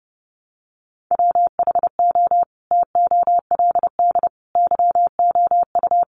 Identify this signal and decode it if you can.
Morse code, heard as a keyed tone, about 30 wpm, 700 Hz. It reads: WHO TOLD YOU
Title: Sentences from Top 400 Words 30wpm